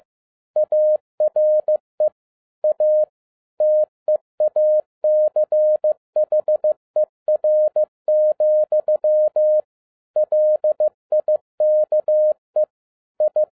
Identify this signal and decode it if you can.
Morse code, heard as a keyed tone, about 15 wpm, 610 Hz. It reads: ARE A TEACHER, LIKE I
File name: 2011-07-mp3 / n07 A LOT HAS CHANGED 15wpm